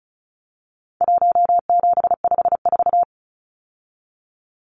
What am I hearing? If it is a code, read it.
Morse code, heard as a keyed tone, about 34 wpm, 700 Hz. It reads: 1754
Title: Numbers - Contest Style - 4-Digit 35wpm